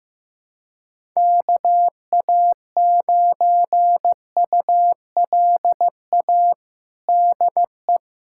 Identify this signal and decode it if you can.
Morse code, heard as a keyed tone, about 15 wpm, 700 Hz. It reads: KA9ULA DE
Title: USA Callsign DE Callsign - 6 Characters - 15wpm